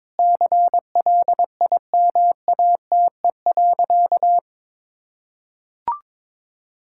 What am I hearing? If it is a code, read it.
Morse code, heard as a keyed tone, about 21 wpm, 700 Hz. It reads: CLIMATE.
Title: Sentences from Top 1600 Words 22wpm